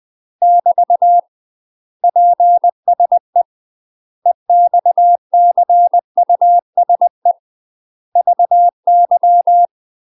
Morse code, heard as a keyed tone, about 20 wpm, 700 Hz. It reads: <BT> PSE EXCUSE VY
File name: UK MASTER QSOs TRAINING / 20WPM / UK MASTER QSOs 0035